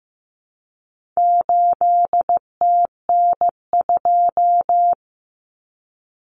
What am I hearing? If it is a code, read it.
Morse code, heard as a keyed tone, about 15 wpm, 700 Hz. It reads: 8TN2
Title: Common Cut Numbers - 4-Digit 15wpm